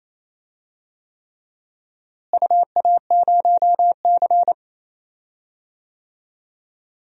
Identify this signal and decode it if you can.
Morse code, heard as a keyed tone, about 28 wpm, 700 Hz. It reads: UA0C
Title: DXCC-100 Call Signs - 4 Characters - Repeated 3x - 28wpm